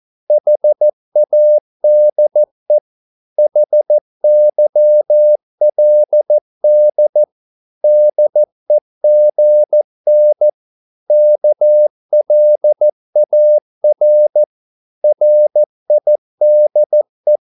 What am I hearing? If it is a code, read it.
Morse code, heard as a keyed tone, about 14 wpm, 600 Hz. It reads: HADE HYLD DEGN KLAR RIDE
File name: Korte ord DK 14wpm